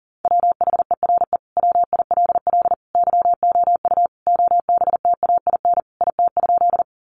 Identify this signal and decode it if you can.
Morse code, heard as a keyed tone, about 40 wpm, 700 Hz. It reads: WHERE WILL YOU OBTAIN IT?
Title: Sentences from Top 1300 Words 40wpm